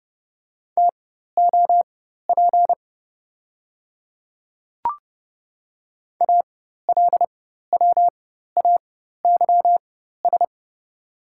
Morse code, heard as a keyed tone, about 32 wpm, spaced as for 14 wpm, 700 Hz. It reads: TOP ALWAYS
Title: 4x Character Spacing - Top 500 Words - 30wpm